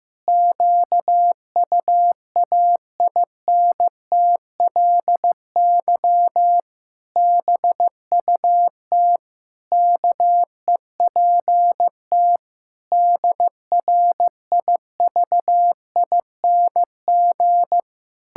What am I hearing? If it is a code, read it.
Morse code, heard as a keyed tone, about 15 wpm, 700 Hz. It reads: QUAINTLY BUT KEPT DRIVING
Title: Pangram Challenge v2 - 15wpm